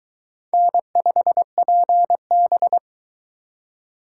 Morse code, heard as a keyed tone, about 22 wpm, 700 Hz. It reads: N5PB